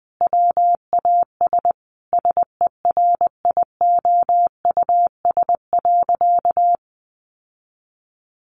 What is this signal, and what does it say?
Morse code, heard as a keyed tone, about 20 wpm, 700 Hz. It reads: WAS SERIOUS.
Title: Sentences from Top 1100 Words 20wpm